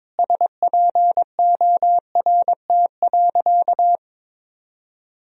Morse code, heard as a keyed tone, about 22 wpm, 700 Hz. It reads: SPORT.
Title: Sentences from Top 1000 Words - Review Part 4 - 22wpm